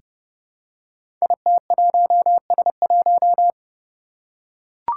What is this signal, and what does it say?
Morse code, heard as a keyed tone, about 30 wpm, 700 Hz. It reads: IT1S1
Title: Five Challenging Letter-Number Combinations 30wpm